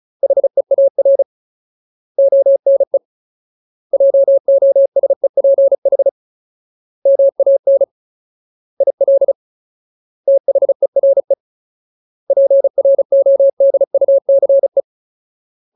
Morse code, heard as a keyed tone, about 34 wpm, 550 Hz. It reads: HEAR ONE JOSEPH MAN IL THERE PRODUCE
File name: Ultimate Word List / 35WPM Ultimate